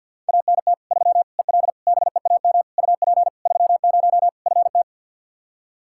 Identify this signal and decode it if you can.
Morse code, heard as a keyed tone, about 52 wpm, 700 Hz. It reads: ANT 3 EL BEAM UP 30 FT
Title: Practice QSO - Exchange 2 - Info - 50wpm